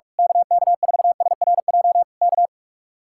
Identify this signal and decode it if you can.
Morse code, heard as a keyed tone, about 44 wpm, 700 Hz. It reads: KK4SRJ K
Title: Speed-Racing Common QSO Phrases - 45wpm